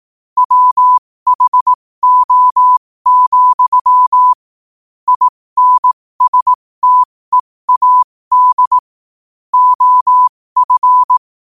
Morse code, heard as a keyed tone, about 18 wpm, spaced as for 15 wpm, 1000 Hz. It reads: WHO, INSTEAD OF
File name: qotdfiles / 15wpm